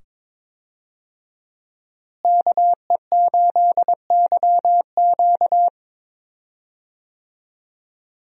Morse code, heard as a keyed tone, about 22 wpm, 700 Hz. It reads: KE8YQ
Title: USA Call Signs - 5 Characters - Repeated 2x - 22wpm